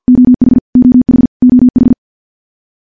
Morse code, heard as a keyed tone, about 50 wpm, 250 Hz. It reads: OH OH OH
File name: morse / morse-training-50WPM